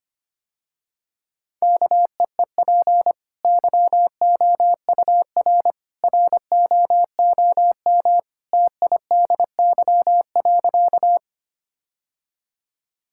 Morse code, heard as a keyed tone, about 25 wpm, 700 Hz. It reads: KEEP YOUR ROOM TIDY.
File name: Easy Sentences with Hints - 25wpm